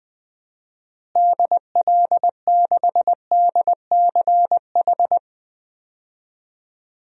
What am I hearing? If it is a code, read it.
Morse code, heard as a keyed tone, about 19 wpm, 700 Hz. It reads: DL6DCH
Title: DXCC-100 Call Signs - 6 Characters - Repeated 3x - 20wpm